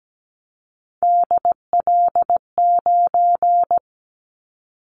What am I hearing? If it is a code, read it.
Morse code, heard as a keyed tone, about 17 wpm, 700 Hz. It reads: DL9
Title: Callsign Prefixes for Top 50 DXCC - 17wpm